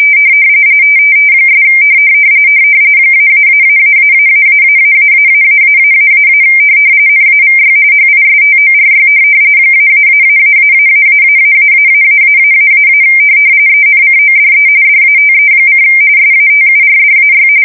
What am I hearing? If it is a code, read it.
RTTY, about 45.45 baud. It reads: ' -------------------- NAVY HISTORY -------------------- ENCODE DATE: SATURDAY
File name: fsk170 / USS-New-Jersey